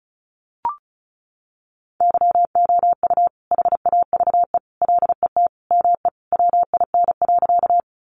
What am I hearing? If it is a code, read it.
Morse code, heard as a keyed tone, about 34 wpm, 700 Hz. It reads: YOU HAVE LET ME WIN.
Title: Sentences from Top 1500 Words 35wpm